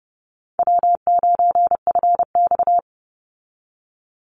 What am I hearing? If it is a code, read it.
Morse code, heard as a keyed tone, about 30 wpm, 700 Hz. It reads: W9FX
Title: Call Signs Repeated 3x - 30wpm